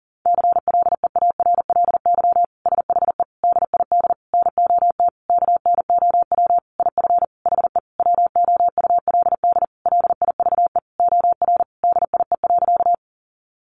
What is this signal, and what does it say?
Morse code, heard as a keyed tone, about 40 wpm, 700 Hz. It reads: CLEARLY SHE DID NOT KNOW IF HE WOULD LIVE OR DIE.
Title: Sentences from Top 1300 Words 40wpm